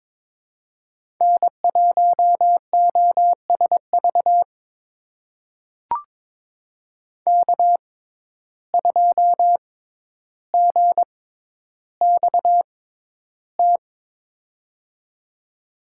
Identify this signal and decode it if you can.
Morse code, heard as a keyed tone, about 22 wpm, spaced as for 11 wpm, 700 Hz. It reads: N1OSV K2GXT
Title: USA Call Signs - 5 Characters - 6x Character Spacing - 22wpm